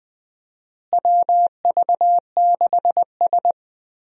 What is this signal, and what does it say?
Morse code, heard as a keyed tone, about 20 wpm, 700 Hz. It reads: WV6S
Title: USA Call Signs - 4 Characters - Repeated 3x - 20wpm